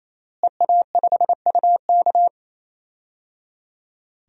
Morse code, heard as a keyed tone, about 28 wpm, 700 Hz. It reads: EA5UK
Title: DXCC-100 Call Signs - 5 Characters - Repeated 3x - 28wpm